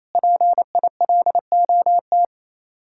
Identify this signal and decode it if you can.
Morse code, heard as a keyed tone, about 34 wpm, 700 Hz. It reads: PILOT
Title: Speed-Racing Common QSO Phrases - 28wpm